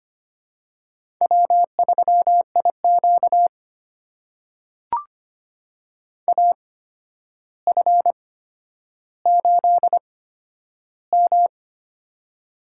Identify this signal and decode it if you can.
Morse code, heard as a keyed tone, about 25 wpm, spaced as for 11 wpm, 700 Hz. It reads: W3IQ AF8M
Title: USA Call Signs - 8x Character Spacing - 25wpm